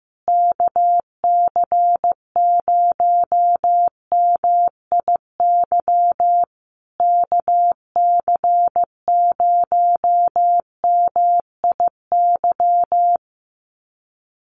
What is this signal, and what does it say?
Morse code, heard as a keyed tone, about 15 wpm, 700 Hz. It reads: KC0MIY KC0MIY
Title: USA Call Signs Repeated 3x 15wpm